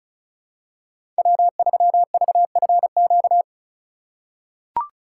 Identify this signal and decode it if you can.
Morse code, heard as a keyed tone, about 36 wpm, 700 Hz. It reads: W3VFQ
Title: USA Call Signs - 5 Characters - 3x Character Spacing - 35wpm